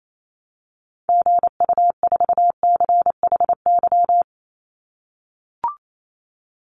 Morse code, heard as a keyed tone, about 28 wpm, 700 Hz. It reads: GU4CHY
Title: Call Signs - 28wpm